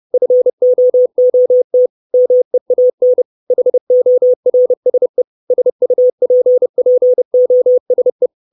Morse code, heard as a keyed tone, about 30 wpm, 500 Hz. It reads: FOOT MEAN HORSE SUPPOSE
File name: 632 Words 30WPM